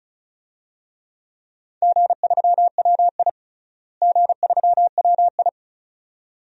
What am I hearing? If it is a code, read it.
Morse code, heard as a keyed tone, about 36 wpm, 700 Hz. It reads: G3WI G3WI
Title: DXCC-100 Call Signs - 4 Characters - Repeated 2x - 35wpm